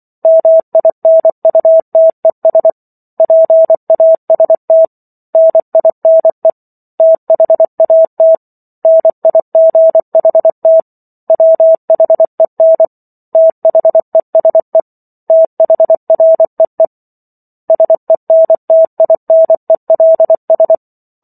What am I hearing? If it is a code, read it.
Morse code, heard as a keyed tone, about 24 wpm, 650 Hz. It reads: MINUTES PAST NINE THAT NIGHT WHEN THESE THREE SENTINELS